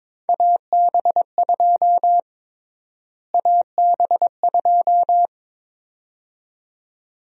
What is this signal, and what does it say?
Morse code, heard as a keyed tone, about 22 wpm, 700 Hz. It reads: AB2 AB2
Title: USA Call Sign Prefixes - Repeated 2x - 22wpm